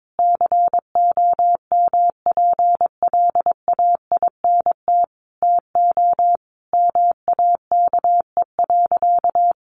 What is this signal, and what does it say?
Morse code, heard as a keyed tone, about 22 wpm, 700 Hz. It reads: COMPLAINT TO MAKE.
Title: Sentences from Top 1400 Words 22wpm